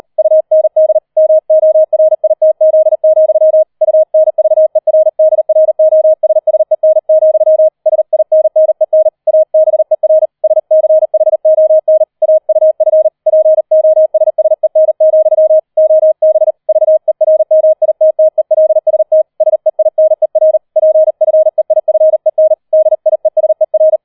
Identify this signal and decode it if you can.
Morse code, heard as a keyed tone, about 38 wpm, 630 Hz. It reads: UND MORITZ, UNVERDROSSEN, SINNEN ABER SCHON AUF POSSEN, OB VERMITTELST SEINER PFEIFEN DIESER
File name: 004 Vierter Streich 40wpm